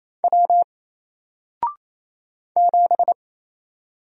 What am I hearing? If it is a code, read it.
Morse code, heard as a keyed tone, about 28 wpm, 700 Hz. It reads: W 7
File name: Single-Letter-Number-28wpm